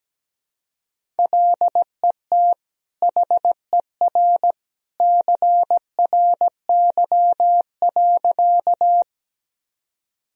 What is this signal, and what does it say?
Morse code, heard as a keyed tone, about 17 wpm, 700 Hz. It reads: LET HER CRY.
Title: Sentences from Top 1000 Words - Review Part 2 - 17wpm